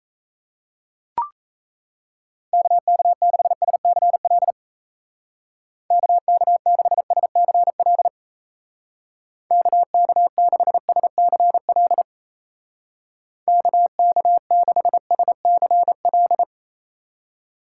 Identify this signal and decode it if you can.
Morse code, heard as a keyed tone, about 37 wpm, 700 Hz. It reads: KK6SCL KK6SCL KK6SCL KK6SCL